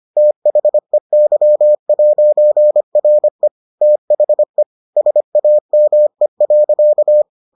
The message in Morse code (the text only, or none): THEY'RE THE SAME.